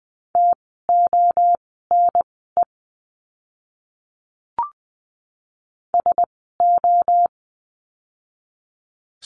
Morse code, heard as a keyed tone, about 20 wpm, spaced as for 14 wpm, 700 Hz. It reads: TONE SO
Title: 068 Lesson 12f - Letter d - Words 20wpm